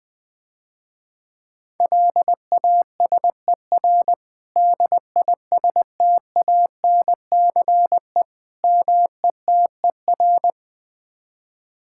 Morse code, heard as a keyed tone, about 21 wpm, 700 Hz. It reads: LASER DISTANCE METER
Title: Difficult Words - Speed-Racing with Hints - 20wpm